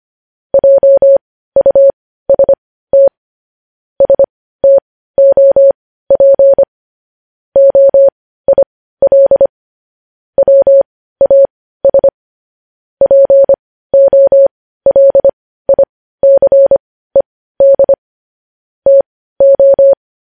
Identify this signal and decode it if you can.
Morse code, heard as a keyed tone, about 25 wpm, spaced as for 15 wpm, 550 Hz. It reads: JUST STOP OIL WAS POLICED TO
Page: News Headlines in Morse Code at 15 WPM